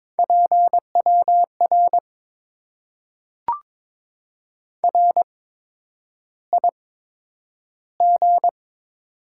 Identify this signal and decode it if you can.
Morse code, heard as a keyed tone, about 22 wpm, spaced as for 9 wpm, 700 Hz. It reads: PWR RIG
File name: Top 90 QSO Elements - 8x Character Spacing - 22wpm